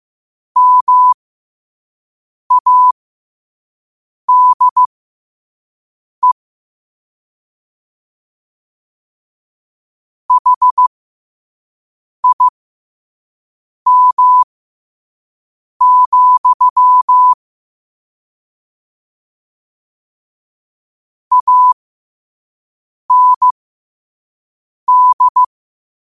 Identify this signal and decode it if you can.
Morse code, heard as a keyed tone, about 15 wpm, spaced as for 5 wpm, 1000 Hz. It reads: MADE HIM, AND
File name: qotdfiles / 5wpm